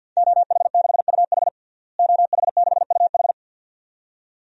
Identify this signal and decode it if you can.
Morse code, heard as a keyed tone, about 46 wpm, 700 Hz. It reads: OH6UH OH6UH
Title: DXCC-100 Call Signs - 5 Characters - Repeated 2x - 50wpm